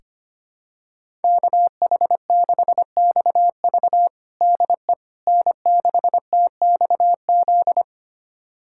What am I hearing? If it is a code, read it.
Morse code, heard as a keyed tone, about 25 wpm, 700 Hz. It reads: KH6XV DE N6TXZ
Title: USA Callsign DE Callsign - 5 Characters - 25wpm